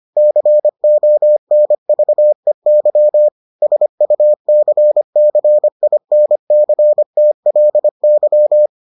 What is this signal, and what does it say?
Morse code, heard as a keyed tone, about 25 wpm, 600 Hz. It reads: CONVEY SUCCINCTLY